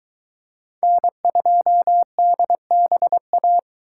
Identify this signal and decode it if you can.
Morse code, heard as a keyed tone, about 22 wpm, 700 Hz. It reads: N2DBA